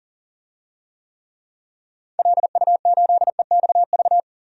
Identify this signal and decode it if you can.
Morse code, heard as a keyed tone, about 40 wpm, 700 Hz. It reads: LU8EXV
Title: DXCC-100 Call Signs - 6 Characters - 40wpm